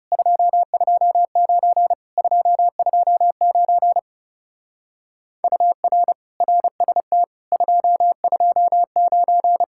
Morse code, heard as a keyed tone, about 37 wpm, 700 Hz. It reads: 229 229 UR RST 229
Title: Speed-Racing Common QSO Phrases - 30wpm